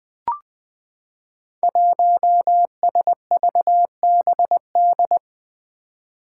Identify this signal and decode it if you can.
Morse code, heard as a keyed tone, about 20 wpm, 700 Hz. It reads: 1SVBD
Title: Five Challenging Letter-Number Combinations 20wpm